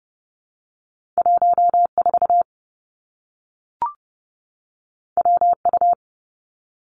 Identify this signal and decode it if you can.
Morse code, heard as a keyed tone, about 30 wpm, 700 Hz. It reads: 1 4 W U
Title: Two Challenging Letter-Number Combinations 30wpm